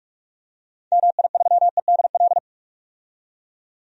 Morse code, heard as a keyed tone, about 46 wpm, 700 Hz. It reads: MI3EDL
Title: DXCC-100 Call Signs - 6 Characters - 45wpm